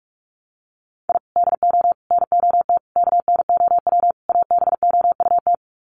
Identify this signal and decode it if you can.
Morse code, heard as a keyed tone, about 44 wpm, 700 Hz. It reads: I DO NOT KNOW ABOUT
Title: Sets of 5 Words 45wpm